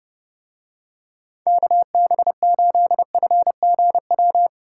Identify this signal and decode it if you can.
Morse code, heard as a keyed tone, about 31 wpm, 700 Hz. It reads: KB8FGW